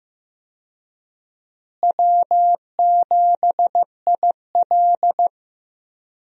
Morse code, heard as a keyed tone, about 15 wpm, 700 Hz. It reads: W7IL
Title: USA Call Signs - 4 Characters - Repeated 2x - 15wpm